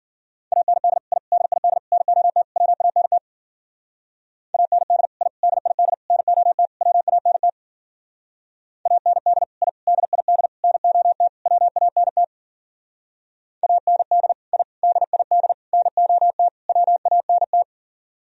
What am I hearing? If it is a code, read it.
Morse code, heard as a keyed tone, about 50 wpm, 700 Hz. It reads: AND I DID NOT WANT AND I DID NOT WANT AND I DID NOT WANT AND I DID NOT WANT
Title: Speed-Racing - Sets of 5 Words 40wpm